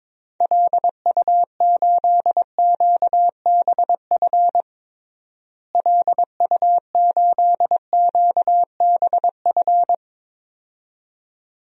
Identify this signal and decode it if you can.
Morse code, heard as a keyed tone, about 22 wpm, 700 Hz. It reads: LU8QBF LU8QBF
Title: DXCC-100 Call Signs - 6 Characters - Repeated 2x - 22wpm